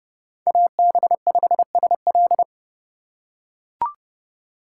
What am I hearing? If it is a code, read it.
Morse code, heard as a keyed tone, about 30 wpm, 700 Hz. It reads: AB5SL
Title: USA Call Signs - 5 Characters - 2x Character Spacing - 30wpm